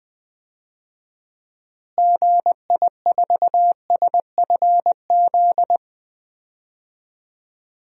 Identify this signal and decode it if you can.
Morse code, heard as a keyed tone, about 20 wpm, 700 Hz. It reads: GI4SFZ
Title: DXCC-100 Call Signs - 6 Characters - Repeated 2x - 20wpm